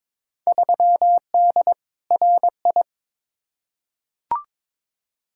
Morse code, heard as a keyed tone, about 22 wpm, 700 Hz. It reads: 3D RI
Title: Field Day Exchanges Encore - 22wpm